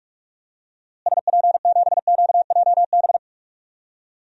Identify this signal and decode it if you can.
Morse code, heard as a keyed tone, about 45 wpm, 700 Hz. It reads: SP7QJB
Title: DXCC-100 Call Signs - 6 Characters - Repeated 3x - 45wpm